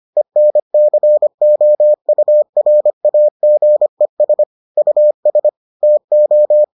Morse code, heard as a keyed tone, about 25 wpm, 600 Hz. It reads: ENCOURAGES US TO